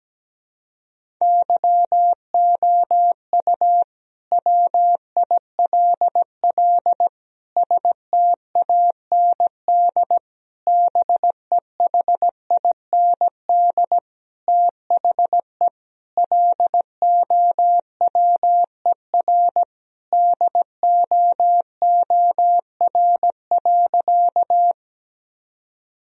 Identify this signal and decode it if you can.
Morse code, heard as a keyed tone, about 17 wpm, 700 Hz. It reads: YOU WILL STAND BEHIND THE LOWER DOOR.
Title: Sentences from Top 1000 Words - Review Part 4 - 17wpm